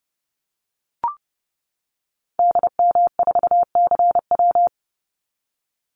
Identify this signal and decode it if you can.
Morse code, heard as a keyed tone, about 30 wpm, 700 Hz. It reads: DM4CW
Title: Call Signs - 30wpm